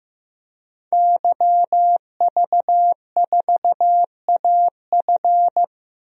Morse code, heard as a keyed tone, about 15 wpm, 700 Hz. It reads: YV4AF
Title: DXCC-100 Call Signs - 5 Characters - 15wpm